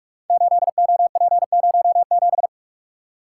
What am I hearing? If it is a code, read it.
Morse code, heard as a keyed tone, about 44 wpm, 700 Hz. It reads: 8OP07
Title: Code Groups - 5 Letter:Numbers - 45wpm